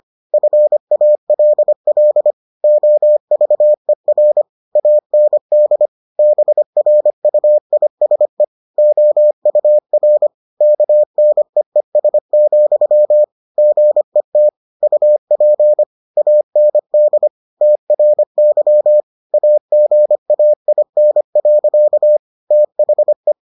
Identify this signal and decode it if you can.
Morse code, heard as a keyed tone, about 25 wpm, 600 Hz. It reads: FALL OVER AND BRUISE OUR KNEES, GET UP AND TRY AGAIN. THE